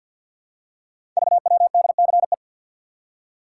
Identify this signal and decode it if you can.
Morse code, heard as a keyed tone, about 45 wpm, 700 Hz. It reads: VWDCE
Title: Code Groups - 5 Letters - 50wpm